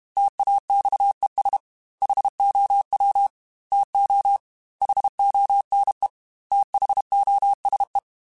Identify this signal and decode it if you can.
Morse code, heard as a keyed tone, about 31 wpm, 800 Hz. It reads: TAXES HOW TO HONE THOSE
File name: T then H Runs / T then H Runs.mp3 / T then H Runs @30